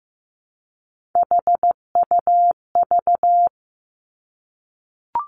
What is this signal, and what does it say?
Morse code, heard as a keyed tone, about 15 wpm, 700 Hz. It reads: HUV
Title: Three Challenging Letter-Number Combinations 15wpm